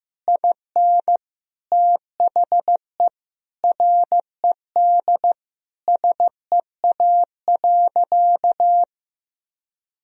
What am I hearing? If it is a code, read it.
Morse code, heard as a keyed tone, about 16 wpm, 700 Hz. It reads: IN THE RED SEA.
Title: Sentences from Top 1000 Words - Review Part 2 - 15wpm